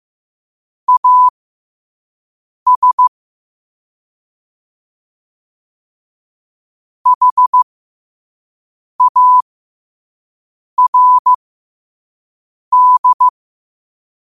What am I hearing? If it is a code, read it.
Morse code, heard as a keyed tone, about 15 wpm, spaced as for 5 wpm, 1000 Hz. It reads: AS HARD